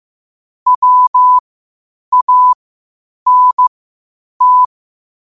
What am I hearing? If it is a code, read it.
Morse code, heard as a keyed tone, about 15 wpm, spaced as for 8 wpm, 1000 Hz. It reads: WANT
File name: qotdfiles / 8wpm